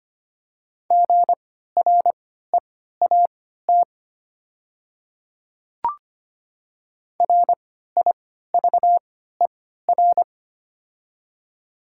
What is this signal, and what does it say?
Morse code, heard as a keyed tone, about 25 wpm, spaced as for 14 wpm, 700 Hz. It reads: GREAT RIVER
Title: 3x Character Spacing - Top 300 Words - 25wpm